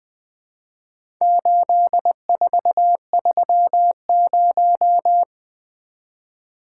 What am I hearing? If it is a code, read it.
Morse code, heard as a keyed tone, about 20 wpm, 700 Hz. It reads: 8430
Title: Four Letter-Number Combinations - 20wpm